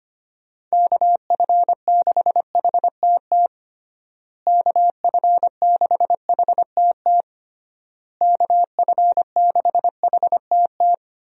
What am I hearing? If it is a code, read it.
Morse code, heard as a keyed tone, about 25 wpm, 700 Hz. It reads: KF6HTT KF6HTT KF6HTT
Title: USA Call Signs - 6 Characters - Repeated 3x - 25wpm